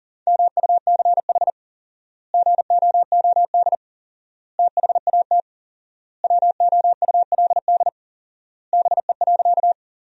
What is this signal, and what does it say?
Morse code, heard as a keyed tone, about 40 wpm, 700 Hz. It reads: MUCH GOOD THAT WOULD BE.
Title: 4x Word-Spacing - Sentences from Top 200 Words - 40wpm